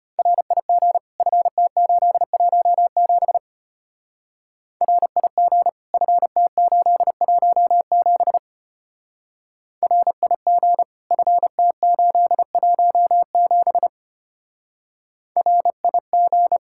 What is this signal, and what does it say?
Morse code, heard as a keyed tone, about 30 wpm, 700 Hz. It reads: RIG FT817 RIG FT817 RIG FT817 RIG
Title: Speed-Racing Common QSO Phrases - 25wpm